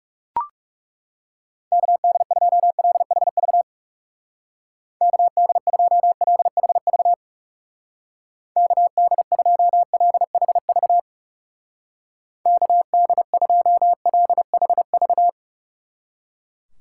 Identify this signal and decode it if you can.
Morse code, heard as a keyed tone, about 37 wpm, 700 Hz. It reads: KD2LHV KD2LHV KD2LHV KD2LHV
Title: Speed Racing - USA 2x3 Call Signs 30wpm